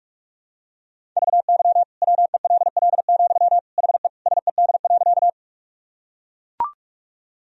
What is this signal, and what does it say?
Morse code, heard as a keyed tone, about 42 wpm, 700 Hz. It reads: VY WELL, HE SED.
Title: Sentences from Top 200 Words and Abbreviations 45wpm